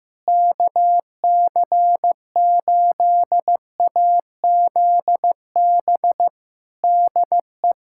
Morse code, heard as a keyed tone, about 15 wpm, 700 Hz. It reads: KC8AZB DE
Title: K1USN SST QSO Practice 15wpm